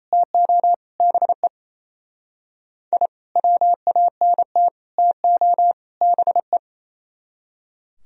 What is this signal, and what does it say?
Morse code, heard as a keyed tone, about 35 wpm, 700 Hz. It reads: TO BE I WANT TO BE
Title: Speed-Racing - Sets of 4 Words 28wpm